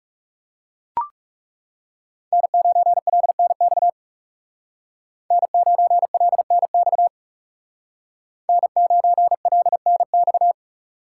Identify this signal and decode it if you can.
Morse code, heard as a keyed tone, about 34 wpm, 700 Hz. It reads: N9LNX N9LNX N9LNX
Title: Speed Racing - USA Call Signs 30wpm